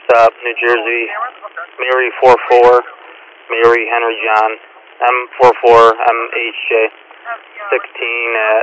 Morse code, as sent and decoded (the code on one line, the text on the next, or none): none